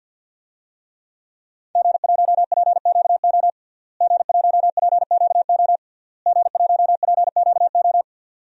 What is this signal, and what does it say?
Morse code, heard as a keyed tone, about 51 wpm, 700 Hz. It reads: G1PQO G1PQO G1PQO
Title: DXCC-100 Call Signs - 5 Characters - Repeated 3x - 50wpm